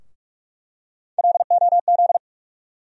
Morse code, heard as a keyed tone, about 42 wpm, 700 Hz. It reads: LOZ
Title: USA Call Sign Postfixes - 3 Letters 45wpm